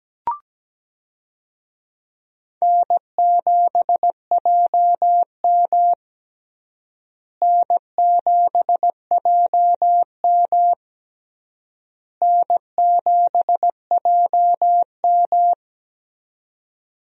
Morse code, heard as a keyed tone, about 17 wpm, 700 Hz. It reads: N7JM N7JM N7JM
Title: USA Call Signs - 4 Characters - Repeated 3x - 17wpm